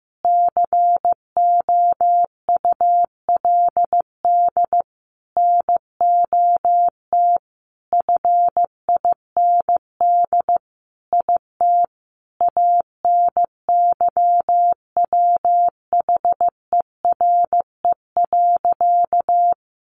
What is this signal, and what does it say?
Morse code, heard as a keyed tone, about 15 wpm, 700 Hz. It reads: COULD NOT FIND IT ANYWHERE.
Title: Sentences from Top 1400 Words 15wpm